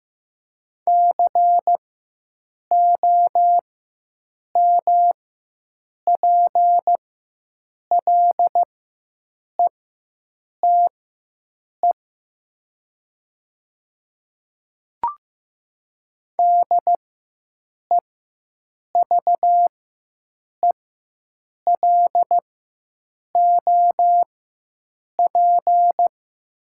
Morse code, heard as a keyed tone, about 15 wpm, spaced as for 7 wpm, 690 Hz. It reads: COMPLETE DEVELOP
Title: 4x Character Spacing - Top 500 Words - 15wpm